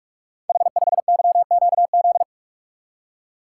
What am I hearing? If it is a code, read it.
Morse code, heard as a keyed tone, about 44 wpm, 700 Hz. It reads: H5YQZ
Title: Code Groups - 5 Letter:Numbers - 45wpm